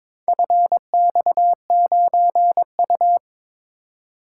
Morse code, heard as a keyed tone, about 22 wpm, 700 Hz. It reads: FX9U
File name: practice / 22wpm / Four Letter-Number Combinations - 22wpm